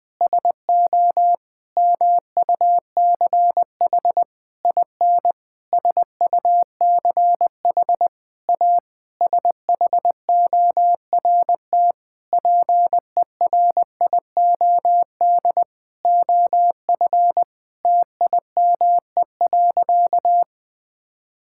Morse code, heard as a keyed tone, about 20 wpm, 700 Hz. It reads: SO MUCH IN SUCH A SHORT PERIOD OF TIME.
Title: Sentences with 7-Word Phrases - 20wpm